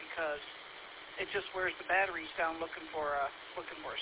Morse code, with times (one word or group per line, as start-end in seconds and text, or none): none